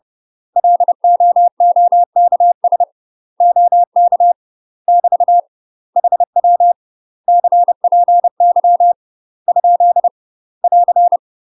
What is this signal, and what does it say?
Morse code, heard as a keyed tone, about 30 wpm, 700 Hz. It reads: LOOKS OK <BT> HW CPY ? <AR>
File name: UK MASTER QSOs TRAINING / 30WPM / UK MASTER QSOs 0007